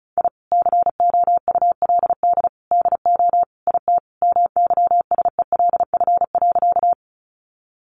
Morse code, heard as a keyed tone, about 34 wpm, 700 Hz. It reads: I COULD DO IT MYSELF.
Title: Sentences from Top 1300 Words 35wpm